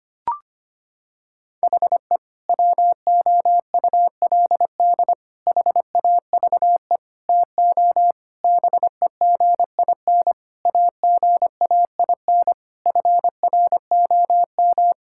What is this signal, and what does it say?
Morse code, heard as a keyed tone, about 25 wpm, 700 Hz. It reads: HE WOULD HAVE TO BEGIN AGAIN FROM
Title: Sentences from Top 1000 Words - Review Part 5 - 25wpm